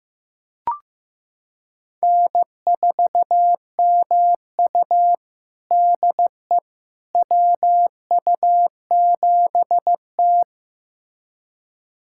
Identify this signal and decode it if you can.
Morse code, heard as a keyed tone, about 15 wpm, 700 Hz. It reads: N4MU DE WU7T
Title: USA Callsign DE Callsign - 4 Characters - 15wpm